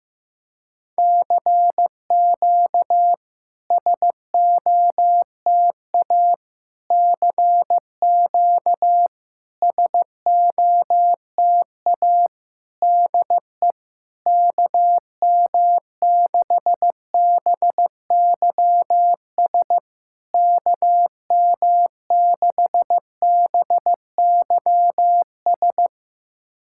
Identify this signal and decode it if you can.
Morse code, heard as a keyed tone, about 15 wpm, 700 Hz. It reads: CQ SOTA CQ SOTA DE KM6BYS KM6BYS
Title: SOTA QSO Practice 15wpm